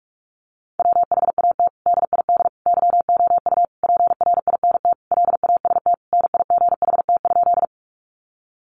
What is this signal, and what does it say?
Morse code, heard as a keyed tone, about 44 wpm, 700 Hz. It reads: WHAT DID YOU PRINT LAST NIGHT?
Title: Sentences from Top 1400 Words 45wpm